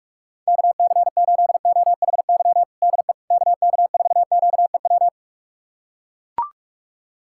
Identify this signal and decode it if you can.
Morse code, heard as a keyed tone, about 44 wpm, 700 Hz. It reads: KC8OHY DE KK4QEW
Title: USA Callsign DE Callsign 45wpm